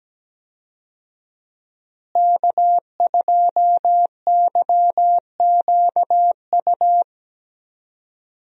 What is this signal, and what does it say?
Morse code, heard as a keyed tone, about 17 wpm, 700 Hz. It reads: K2YQU
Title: USA Call Signs - 5 Characters - Repeated 3x - 17wpm